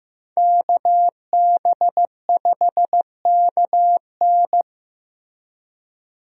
Morse code, heard as a keyed tone, about 15 wpm, 700 Hz. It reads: KB5KN
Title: USA Call Signs - 5 Characters - Repeated 2x - 15wpm